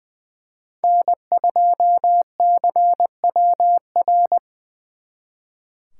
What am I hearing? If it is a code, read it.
Morse code, heard as a keyed tone, about 24 wpm, 700 Hz. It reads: N2CWR
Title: Speed Racing - USA 1x3 Call Signs 20wpm